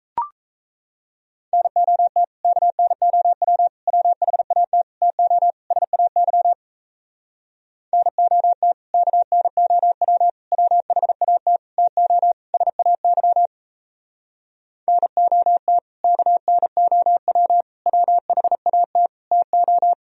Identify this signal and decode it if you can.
Morse code, heard as a keyed tone, about 37 wpm, 700 Hz. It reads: NOT KNOW WHAT TO SAY NOT KNOW WHAT TO SAY NOT KNOW WHAT TO